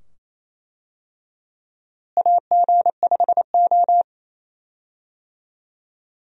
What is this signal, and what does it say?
Morse code, heard as a keyed tone, about 28 wpm, 700 Hz. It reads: AG5O
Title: USA Call Signs - 4 Characters - Repeated 2x - 28wpm